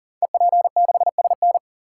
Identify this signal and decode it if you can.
Morse code, heard as a keyed tone, about 42 wpm, 700 Hz. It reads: EP6SN
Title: Code Groups - 5 Letter:Numbers - 40wpm